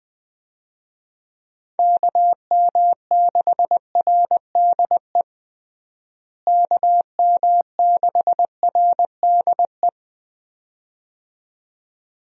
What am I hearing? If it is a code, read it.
Morse code, heard as a keyed tone, about 20 wpm, 700 Hz. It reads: KM6RDE KM6RDE
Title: USA Call Signs - 6 Characters - Repeated 2x - 20wpm